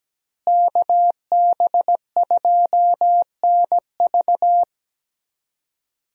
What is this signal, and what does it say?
Morse code, heard as a keyed tone, about 17 wpm, 700 Hz. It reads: KB2NV
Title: USA Call Signs - 5 Characters - Repeated 2x - 17wpm